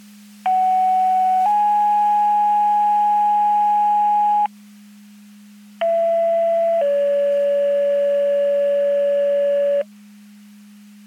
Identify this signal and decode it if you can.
Two-tone page, heard as a one-second tone, then a three-second tone, about 750 then 830 Hz, about 670 then 570 Hz. It is heard from 0.5 to 9.8 s.
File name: Portage 154.385 Paging